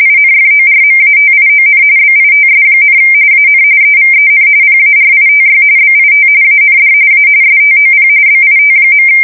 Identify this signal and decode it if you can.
RTTY, about 45.45 baud. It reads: INFORMATION BITS, AN EVEN PARITY BIT AND A STOP BIT.
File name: fsk170 / teletype-corp